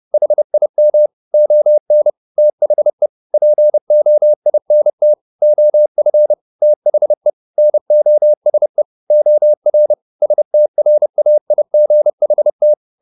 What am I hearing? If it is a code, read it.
Morse code, heard as a keyed tone, about 30 wpm, 600 Hz. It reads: HIM ON THE POINT OF THE NOSE OR STRAIGHT